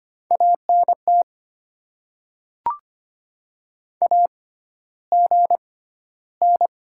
Morse code, heard as a keyed tone, about 25 wpm, spaced as for 13 wpm, 700 Hz. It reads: ANT AGN
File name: Top 90 QSO Elements - 6x Character Spacing - 25wpm